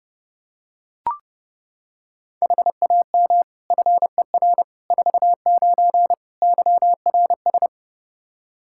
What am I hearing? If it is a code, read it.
Morse code, heard as a keyed tone, about 32 wpm, 700 Hz. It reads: HAM FER 49 YRS
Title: K1USN SST QSO Practice 30wpm